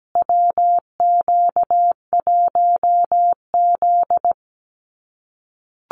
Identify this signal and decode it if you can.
Morse code, heard as a keyed tone, about 17 wpm, 700 Hz. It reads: WQ1Z
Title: USA Call Signs Repeated 2x 17wpm